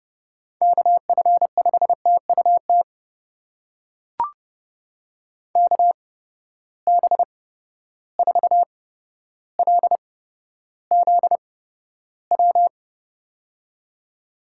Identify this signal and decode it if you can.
Morse code, heard as a keyed tone, about 30 wpm, spaced as for 13 wpm, 700 Hz. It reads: KF5TUT KB4LZW
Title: USA Call Signs - 6 Characters - 8x Character Spacing - 30wpm